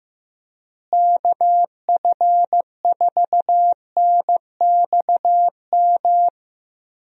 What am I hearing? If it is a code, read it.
Morse code, heard as a keyed tone, about 15 wpm, 700 Hz. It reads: KF4NXM
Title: USA Call Signs - 6 Characters - Repeated 2x - 15wpm